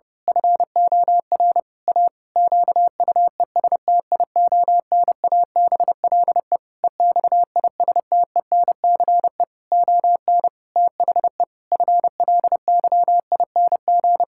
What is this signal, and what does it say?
Morse code, heard as a keyed tone, about 30 wpm, 700 Hz. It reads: FOR A QUESTIONABLE EXISTENCE ON THE FLYING
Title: Pangram Challenge v2 - 30wpm